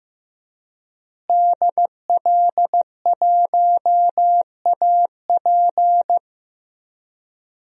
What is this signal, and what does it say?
Morse code, heard as a keyed tone, about 15 wpm, 700 Hz. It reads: DL1AP
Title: DXCC-100 Call Signs - 5 Characters - Repeated 2x - 15wpm